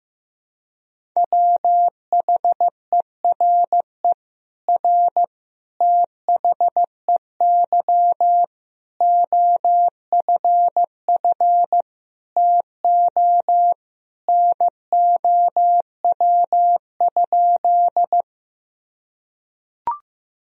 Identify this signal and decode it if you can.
Morse code, heard as a keyed tone, about 15 wpm, 700 Hz. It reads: WHERE R THEY OFF TO NOW?
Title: Sentences from Top 200 Words and Abbreviations 15wpm